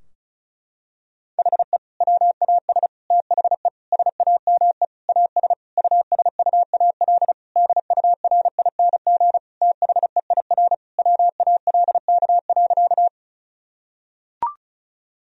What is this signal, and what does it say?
Morse code, heard as a keyed tone, about 34 wpm, 700 Hz. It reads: HE WAS THE SAME AS USUAL DURING THEIR WALK.
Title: Sentences from Top 1000 Words - Review Part 2 - 35wpm